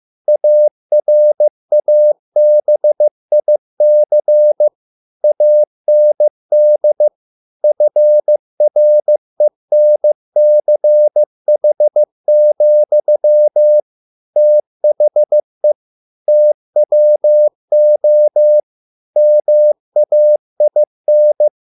Morse code, heard as a keyed tone, about 15 wpm, 600 Hz. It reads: ARABIC AND FRENCH, THE TWO MAIN